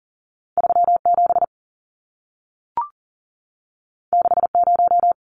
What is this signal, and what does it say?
Morse code, heard as a keyed tone, about 40 wpm, 700 Hz. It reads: 37 60
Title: Numbers - 2-Digit 40wpm